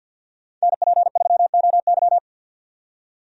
Morse code, heard as a keyed tone, about 53 wpm, 700 Hz. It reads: NP3OY
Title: USA Call Signs - 5 Characters - 50wpm